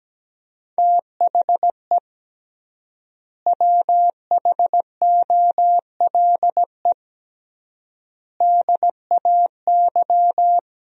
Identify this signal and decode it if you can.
Morse code, heard as a keyed tone, about 17 wpm, 700 Hz. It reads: THE WHOLE DAY
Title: 3x Word-Spacing - Sentences from Top 500 Words - 17wpm